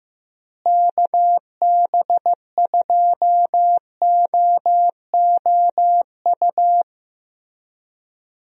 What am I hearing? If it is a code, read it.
Morse code, heard as a keyed tone, about 15 wpm, 700 Hz. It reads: KB2OOU
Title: USA Call Signs - 6 Characters - Repeated 2x - 15wpm